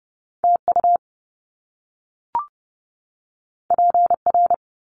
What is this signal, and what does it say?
Morse code, heard as a keyed tone, about 30 wpm, 700 Hz. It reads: T U P R
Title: Top Two Letter Combinations in Words 30wpm